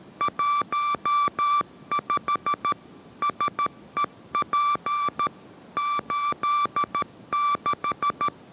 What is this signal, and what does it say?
Morse code, heard as a keyed tone, about 13 wpm, 1170 Hz. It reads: 15 SEP 86